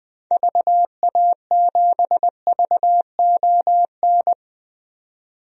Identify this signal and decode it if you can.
Morse code, heard as a keyed tone, about 20 wpm, 700 Hz. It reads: VA7VON
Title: DXCC-100 Call Signs - 6 Characters - 20wpm